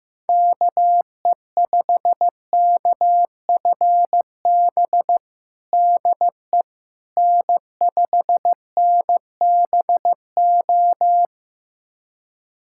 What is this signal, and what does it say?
Morse code, heard as a keyed tone, about 15 wpm, 700 Hz. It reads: KE5KFB DE N5NBO
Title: USA Callsign DE Callsign 15wpm